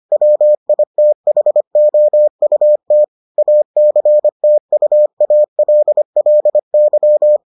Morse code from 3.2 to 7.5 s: ACTUALLY

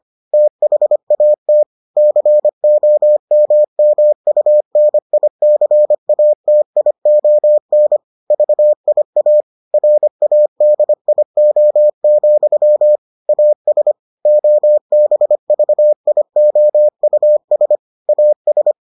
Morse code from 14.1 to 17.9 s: OBVIOUS

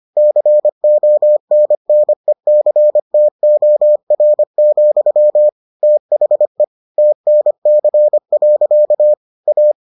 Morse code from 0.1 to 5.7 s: CONNECTOR,

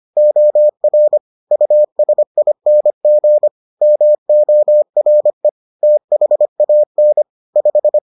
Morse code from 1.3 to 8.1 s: USING MORE THAN 5

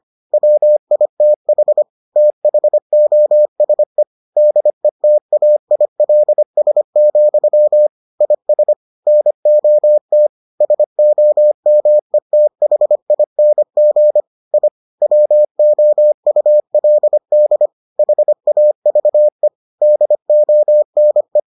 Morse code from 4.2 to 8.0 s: DETAILS,

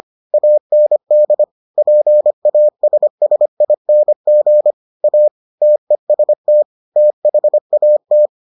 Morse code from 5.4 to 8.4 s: TEST THAT

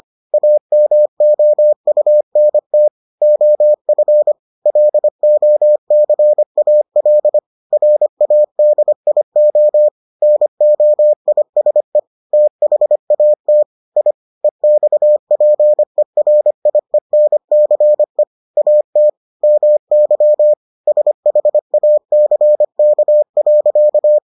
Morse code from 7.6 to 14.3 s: RADIO NOISE THAT I

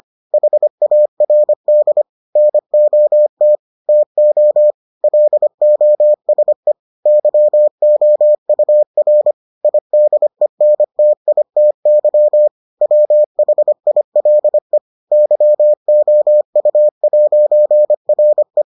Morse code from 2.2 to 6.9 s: NOT TO LOSE

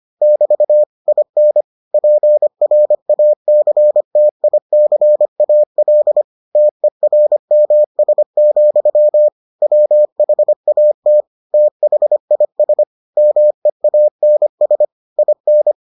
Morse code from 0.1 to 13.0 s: <BT> IN PRACTICAL TERMS, WHAT THIS

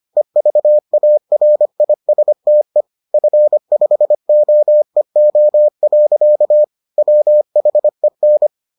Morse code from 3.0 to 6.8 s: F5OEO.

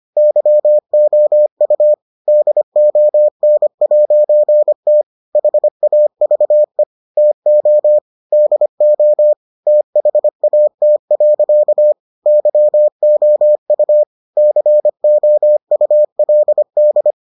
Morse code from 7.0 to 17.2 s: TO DO THAT. YOU COULD

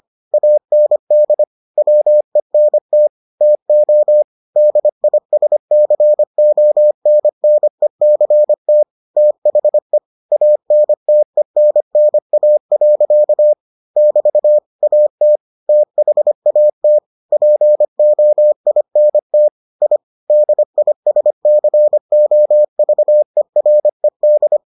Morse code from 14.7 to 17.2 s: AT THAT

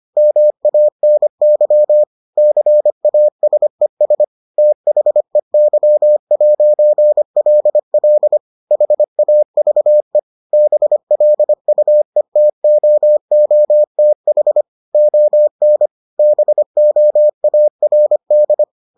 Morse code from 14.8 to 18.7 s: ON BOARD

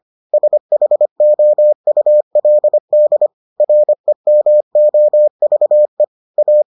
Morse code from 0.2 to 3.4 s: SHOULD

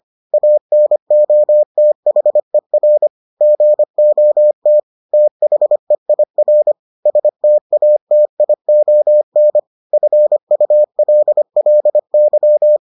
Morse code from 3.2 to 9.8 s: GOT THEIR STATION

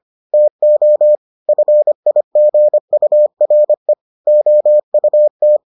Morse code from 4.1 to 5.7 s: OUT

